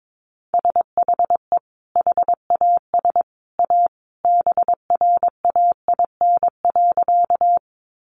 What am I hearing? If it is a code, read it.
Morse code, heard as a keyed tone, about 22 wpm, 700 Hz. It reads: SHE HAS A BRAIN.
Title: Sentences from Top 1100 Words 22wpm